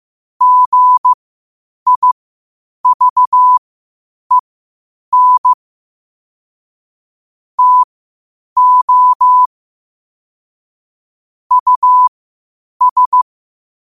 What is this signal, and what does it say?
Morse code, heard as a keyed tone, about 15 wpm, spaced as for 8 wpm, 1000 Hz. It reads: GIVEN TO US
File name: qotdfiles / 8wpm